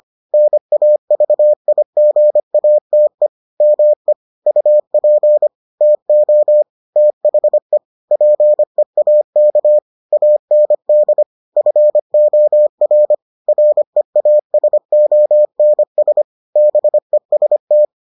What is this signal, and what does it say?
Morse code, heard as a keyed tone, about 25 wpm, 600 Hz. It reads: NAVIGATE ME UP TO THE PEAK AND FOR REASONS BEST